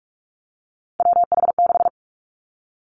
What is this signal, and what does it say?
Morse code, heard as a keyed tone, about 46 wpm, 700 Hz. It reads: WH6
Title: Three Challenging Letter-Number Combinations 45wpm